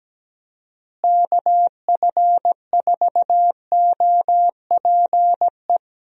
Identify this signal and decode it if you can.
Morse code, heard as a keyed tone, about 17 wpm, 700 Hz. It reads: KF4OPE
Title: USA Call Signs - 6 Characters - Repeated 2x - 17wpm